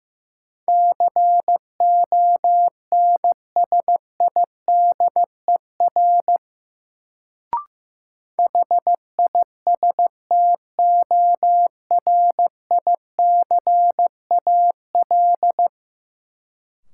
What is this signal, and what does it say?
Morse code, heard as a keyed tone, about 16 wpm, 700 Hz. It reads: CONSIDER HISTORICAL
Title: Sentences from Top 600 Words - New Word Drill - 15wpm